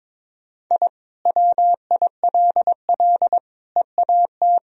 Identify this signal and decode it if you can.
Morse code, heard as a keyed tone, about 22 wpm, 700 Hz. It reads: I WILL EAT
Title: Sentences from Top 1000 Words - Review Part 1 - 22wpm